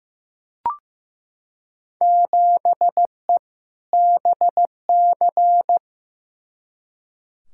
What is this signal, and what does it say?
Morse code, heard as a keyed tone, about 15 wpm, 700 Hz. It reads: 7E BC
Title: Field Day Exchanges Encore - 15wpm